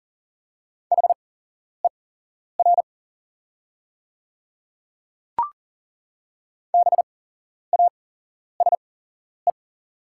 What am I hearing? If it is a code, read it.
Morse code, heard as a keyed tone, about 38 wpm, spaced as for 11 wpm, 690 Hz. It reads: HER BASE
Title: 8x Character Spacing - Top 300 Words - 40wpm